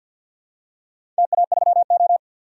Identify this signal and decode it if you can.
Morse code, heard as a keyed tone, about 52 wpm, 700 Hz. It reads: TA3O
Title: DXCC-100 Call Signs - 4 Characters - 50wpm